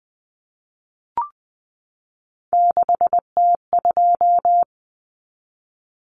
Morse code, heard as a keyed tone, about 20 wpm, 700 Hz. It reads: 6T2
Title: Common Cut Numbers - 3-Digit 20wpm